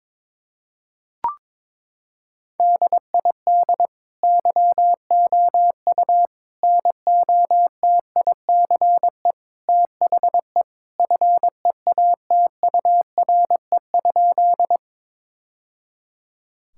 Morse code, heard as a keyed tone, about 22 wpm, 700 Hz. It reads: DID YOU NOTICE THE FEATURE?
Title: Sentences from Top 1000 Words - Review Part 5 - 22wpm